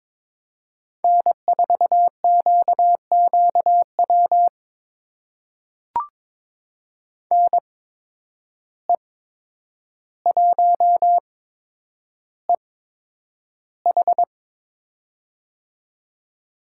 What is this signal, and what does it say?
Morse code, heard as a keyed tone, about 22 wpm, spaced as for 9 wpm, 700 Hz. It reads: N4QQW NE1EH
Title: USA Call Signs - 5 Characters - 8x Character Spacing - 22wpm